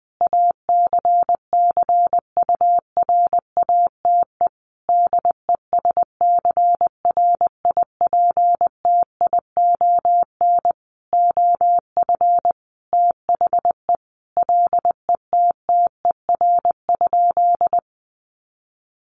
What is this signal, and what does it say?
Morse code, heard as a keyed tone, about 20 wpm, 700 Hz. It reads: ACCURATE DESCRIPTION OF THE LETTER?
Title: Sentences from Top 1300 Words 20wpm